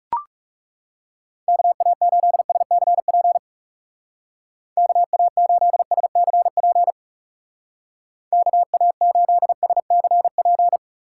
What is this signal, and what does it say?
Morse code, heard as a keyed tone, about 38 wpm, 700 Hz. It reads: KA8SCP KA8SCP KA8SCP